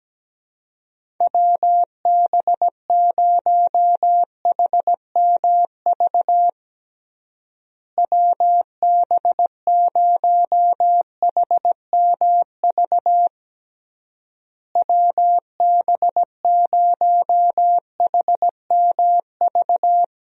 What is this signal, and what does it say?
Morse code, heard as a keyed tone, about 17 wpm, 700 Hz. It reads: WB0HMV WB0HMV WB0HMV
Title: USA Call Signs - 6 Characters - Repeated 3x - 17wpm